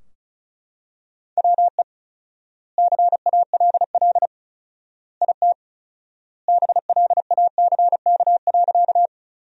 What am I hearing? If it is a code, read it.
Morse code, heard as a keyed tone, about 34 wpm, 700 Hz. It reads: WE CALL IT BLACK.
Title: 4x Word-Spacing - Sentences from Top 400 Words - 35wpm